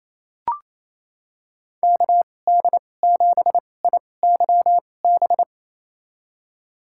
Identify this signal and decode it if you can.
Morse code, heard as a keyed tone, about 28 wpm, spaced as for 20 wpm, 700 Hz. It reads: KD7IYB
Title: USA Call Signs - 6 Characters - 2x Character Spacing - 28wpm